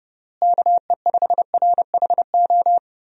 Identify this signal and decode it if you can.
Morse code, heard as a keyed tone, about 28 wpm, 700 Hz. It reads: KE5RHO
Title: USA Call Signs - 6 Characters - Repeated 3x - 30wpm